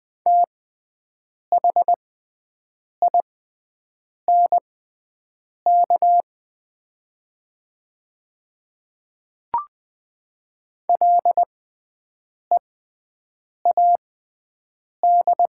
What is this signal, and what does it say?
Morse code, heard as a keyed tone, about 20 wpm, spaced as for 7 wpm, 700 Hz. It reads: THINK LEAD
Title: 6x Character Spacing - Top 500 Words - 20wpm